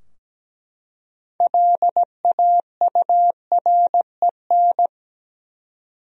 Morse code, heard as a keyed tone, about 17 wpm, 700 Hz. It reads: LAUREN
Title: USA Common Names - 17wpm